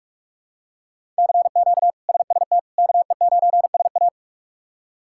Morse code, heard as a keyed tone, about 46 wpm, 700 Hz. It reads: CQ SST KE9SA
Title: K1USN SST QSO Practice - 45wpm